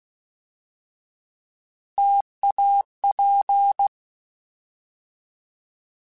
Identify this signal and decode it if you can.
Morse code, heard as a keyed tone, about 16 wpm, 800 Hz. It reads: TAP